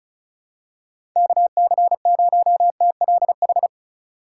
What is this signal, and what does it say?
Morse code, heard as a keyed tone, about 36 wpm, 690 Hz. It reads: KC0TLH
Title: USA Call Signs - 6 Characters - 6x Character Spacing - 35wpm